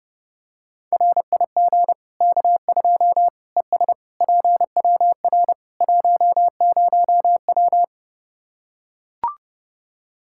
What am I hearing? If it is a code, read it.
Morse code, heard as a keyed tone, about 30 wpm, 700 Hz. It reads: RIG K2 ES PWR 10W
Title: Practice QSO - Exchange 2 - Info - 30wpm